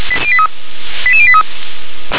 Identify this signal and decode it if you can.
Five-tone selective call (ZVEI), heard as quick runs of five tones, tones of about 70 ms. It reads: 88083